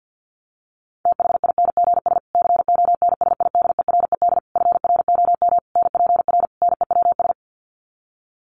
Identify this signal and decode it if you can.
Morse code, heard as a keyed tone, about 50 wpm, 700 Hz. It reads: THINGS CONSIDERED FROM NPR NEWS
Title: Sets of 5 Words Encore 50wpm